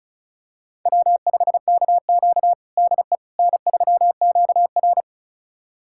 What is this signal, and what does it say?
Morse code, heard as a keyed tone, about 36 wpm, 700 Hz. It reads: W5KQ DE N3QR
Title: USA Callsign DE Callsign - 4 Characters - 35wpm